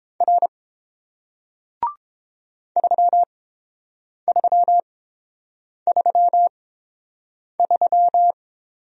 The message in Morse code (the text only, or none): R 3333